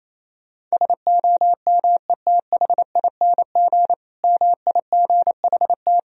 Morse code, heard as a keyed tone, about 27 wpm, 700 Hz. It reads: SOMETHING MIGHT
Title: Sentences from Top 1000 Words - Review Part 1 - 28wpm